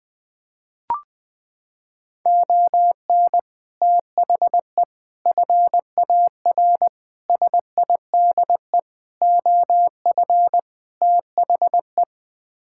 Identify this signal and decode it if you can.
Morse code, heard as a keyed tone, about 20 wpm, 700 Hz. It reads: ON THE FAR SIDE OF THE
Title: Sets of 6 Words - 20wpm